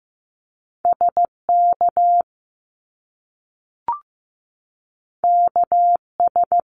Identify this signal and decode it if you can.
Morse code, heard as a keyed tone, about 15 wpm, 700 Hz. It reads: SK KS
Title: Top Two Letter Combinations in Words 15wpm